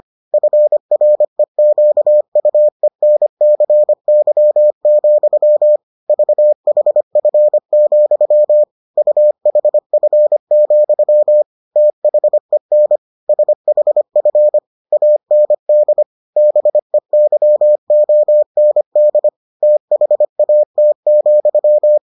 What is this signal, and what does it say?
Morse code, heard as a keyed tone, about 25 wpm, 600 Hz. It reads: FREQUENCY, VHF, UHF, THEN SHF AND BEYOND THAT,